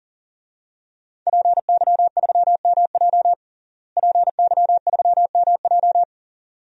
Morse code, heard as a keyed tone, about 40 wpm, 700 Hz. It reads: PY3MJ PY3MJ
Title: DXCC-100 Call Signs - 5 Characters - Repeated 2x - 40wpm